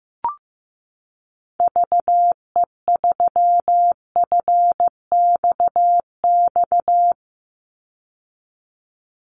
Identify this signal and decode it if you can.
Morse code, heard as a keyed tone, about 15 wpm, 700 Hz. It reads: VE3FXX
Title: Call Signs Repeated 3x - 15wpm